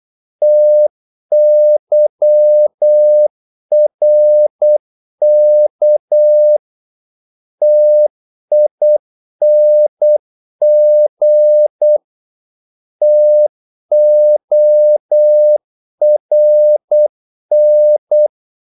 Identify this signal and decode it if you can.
Morse code, heard as a keyed tone, about 8 wpm, 600 Hz. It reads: TYRK TING TORN
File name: Korte ord DK 8wpm